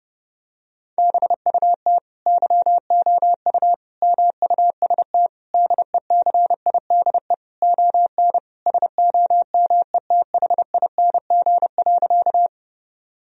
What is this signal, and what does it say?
Morse code, heard as a keyed tone, about 30 wpm, 700 Hz. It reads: BUT YOU MUST DECIDE ON SOMETHING.
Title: Sentences from Top 1000 Words - Review Part 3 - 30wpm